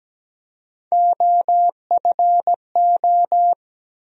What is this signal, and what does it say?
Morse code, heard as a keyed tone, about 17 wpm, 700 Hz. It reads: OFO
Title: USA Call Sign Postfixes - 3 Letters 17wpm